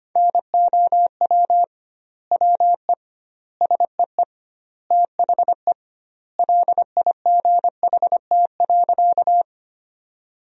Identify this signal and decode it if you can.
Morse code, heard as a keyed tone, about 25 wpm, 700 Hz. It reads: NOW WE SEE THE LIGHT.
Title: 2x Word-Spacing - Sentences from Top 200 Words - 25wpm